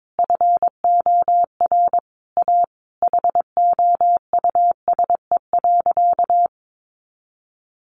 Morse code, heard as a keyed tone, about 22 wpm, 700 Hz. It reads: FOR A HOUSE.
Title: Sentences from Top 600 Words 22wpm